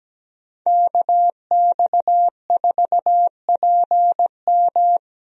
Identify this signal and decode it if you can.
Morse code, heard as a keyed tone, about 17 wpm, 700 Hz. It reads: KX4PM
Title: USA Call Signs - 5 Characters - Repeated 3x - 17wpm